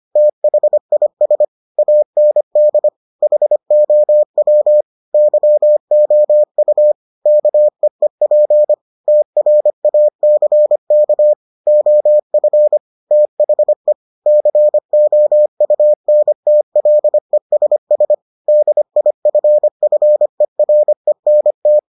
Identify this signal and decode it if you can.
Morse code, heard as a keyed tone, about 25 wpm, 600 Hz. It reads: THIS AND HOW YOU KEEP TRACK OF THE COUNTLESS DIFFERENT